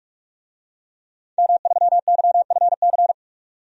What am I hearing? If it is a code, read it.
Morse code, heard as a keyed tone, about 45 wpm, 700 Hz. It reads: M3YFC
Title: DXCC-100 Call Signs - 5 Characters - 45wpm